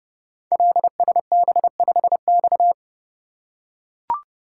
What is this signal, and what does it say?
Morse code, heard as a keyed tone, about 29 wpm, 700 Hz. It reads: LSB5X
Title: Code Groups - 5 Letter:Numbers - 30wpm